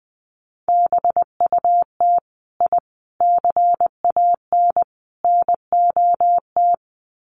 Morse code, heard as a keyed tone, about 20 wpm, 700 Hz. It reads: BUT I CAN NOT
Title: Sets of 4 Words 20wpm